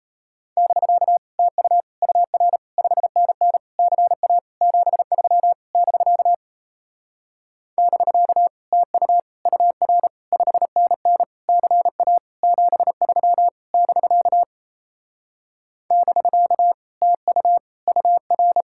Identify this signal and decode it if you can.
Morse code, heard as a keyed tone, about 38 wpm, 700 Hz. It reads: <BK> TU UR 5NN CA 73 <BK> <BK> TU UR 5NN CA 73 <BK> <BK> TU UR